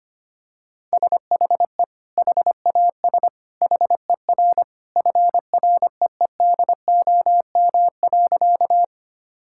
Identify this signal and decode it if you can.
Morse code, heard as a keyed tone, about 25 wpm, 700 Hz. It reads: SHE HAS HER FREEDOM.
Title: Sentences from Top 1000 Words - Review Part 5 - 25wpm